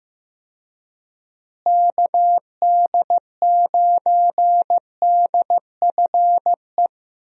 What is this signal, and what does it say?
Morse code, heard as a keyed tone, about 15 wpm, 700 Hz. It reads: KD9DFE
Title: USA Call Signs - 6 Characters - Repeated 2x - 15wpm